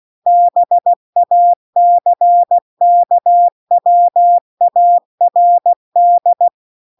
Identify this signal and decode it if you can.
Morse code, heard as a keyed tone, about 16 wpm, 700 Hz. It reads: BACKWARD